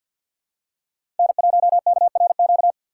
Morse code, heard as a keyed tone, about 52 wpm, 700 Hz. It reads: N1KRQ
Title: USA Call Signs - 5 Characters - 4x Character Spacing - 50wpm